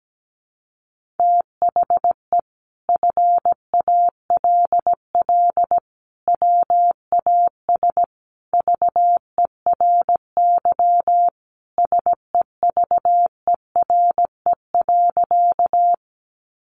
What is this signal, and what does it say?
Morse code, heard as a keyed tone, about 17 wpm, 700 Hz. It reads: THE FALL WAS VERY SEVERE.
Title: Sentences from Top 1400 Words 17wpm